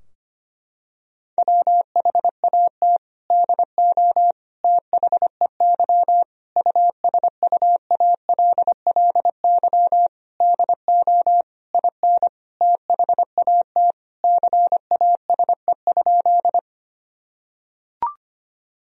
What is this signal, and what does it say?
Morse code, heard as a keyed tone, about 25 wpm, 700 Hz. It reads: WHAT DO THEY USUALLY DO IN THAT CASE?
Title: Sentences from Top 1000 Words - Review Part 3 - 25wpm